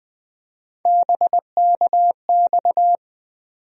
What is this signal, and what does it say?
Morse code, heard as a keyed tone, about 20 wpm, 700 Hz. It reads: BKX
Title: USA Call Sign Postfixes - 3 Letters 20wpm